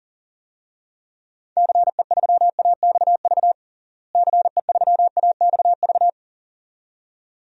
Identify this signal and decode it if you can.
Morse code, heard as a keyed tone, about 40 wpm, 700 Hz. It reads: CE3AXV CE3AXV
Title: DXCC-100 Call Signs - 6 Characters - Repeated 2x - 40wpm